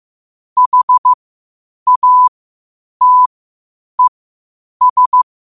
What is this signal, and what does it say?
Morse code, heard as a keyed tone, about 15 wpm, spaced as for 8 wpm, 1000 Hz. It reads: HATES
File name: qotdfiles / 8wpm